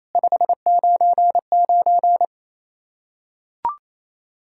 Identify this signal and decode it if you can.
Morse code, heard as a keyed tone, about 28 wpm, 700 Hz. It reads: 599
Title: Practice QSO - Exchange 1 - Beginning - 28wpm